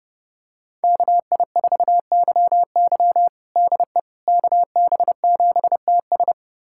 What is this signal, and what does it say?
Morse code, heard as a keyed tone, about 30 wpm, 700 Hz. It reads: KI4YY DE KB7TS
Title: USA Callsign DE Callsign - 5 Characters - 30wpm